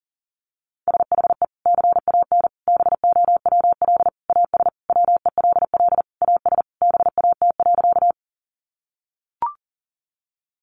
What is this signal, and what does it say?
Morse code, heard as a keyed tone, about 40 wpm, 700 Hz. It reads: SHE CAN BOWL AS WELL AS BAT.
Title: Sentences from Top 1500 Words 40wpm